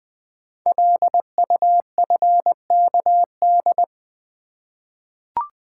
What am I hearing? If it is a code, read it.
Morse code, heard as a keyed tone, about 21 wpm, 700 Hz. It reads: LUFKD
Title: Code Groups - 5 Letters - 20wpm